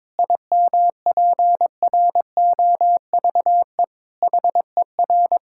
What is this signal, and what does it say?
Morse code, heard as a keyed tone, about 21 wpm, 700 Hz. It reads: IMPROVE HER
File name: Difficult Sentences with Hints - 22wpm